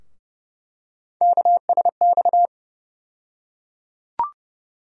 Morse code, heard as a keyed tone, about 32 wpm, 700 Hz. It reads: KSX
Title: USA Call Sign Postfixes - 3 Letters 30wpm